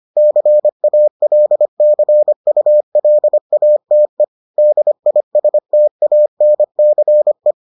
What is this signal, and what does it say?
Morse code, heard as a keyed tone, about 25 wpm, 600 Hz. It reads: CALCULATE DISTANCE